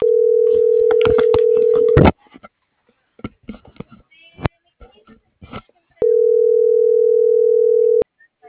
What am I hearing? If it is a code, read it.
Ringback tone (North America), 440 + 480 Hz.